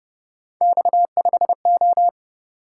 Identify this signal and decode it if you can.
Morse code, heard as a keyed tone, about 29 wpm, 700 Hz. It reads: X5O
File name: Speed-Racing - Three Letter-Number Combinations - 25wpm